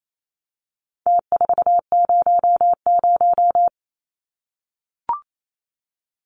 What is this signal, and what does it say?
Morse code, heard as a keyed tone, about 28 wpm, 700 Hz. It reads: T400
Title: Numbers - Contest Style - 4-Digit 28wpm